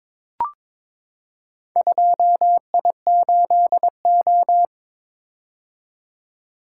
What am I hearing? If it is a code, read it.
Morse code, heard as a keyed tone, about 21 wpm, 700 Hz. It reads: 2I8O
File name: practice / 22wpm / Four Letter-Number Combinations - 22wpm